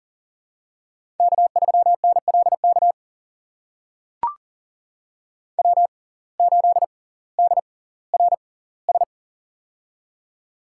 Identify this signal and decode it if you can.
Morse code, heard as a keyed tone, about 39 wpm, spaced as for 20 wpm, 700 Hz. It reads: K3NLK W8DRS
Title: USA Call Signs - 5 Characters - 6x Character Spacing - 40wpm